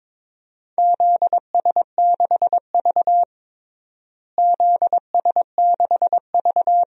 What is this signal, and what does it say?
Morse code, heard as a keyed tone, about 21 wpm, 700 Hz. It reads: ZS6V ZS6V
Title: DXCC-100 Call Signs - 4 Characters - Repeated 3x - 22wpm